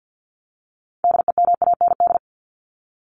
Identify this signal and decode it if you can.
Morse code, heard as a keyed tone, about 50 wpm, 700 Hz. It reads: DEMAND
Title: Top 501-750 Words 50wpm